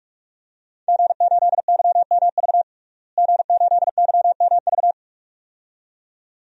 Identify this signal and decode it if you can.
Morse code, heard as a keyed tone, about 45 wpm, 700 Hz. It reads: G8YMV G8YMV
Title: DXCC-100 Call Signs - 5 Characters - Repeated 2x - 45wpm